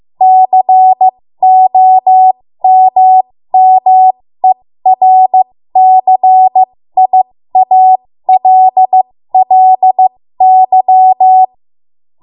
Morse code, heard as a keyed tone, about 15 wpm, spaced as for 13 wpm, 750 Hz. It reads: COMMERCIALLY
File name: Morse / 13WPM